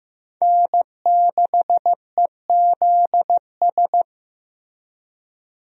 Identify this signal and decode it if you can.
Morse code, heard as a keyed tone, about 15 wpm, 700 Hz. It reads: N6EZS
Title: USA Call Signs - 5 Characters - Repeated 2x - 15wpm